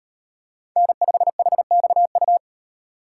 Morse code, heard as a keyed tone, about 38 wpm, 700 Hz. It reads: N5HXU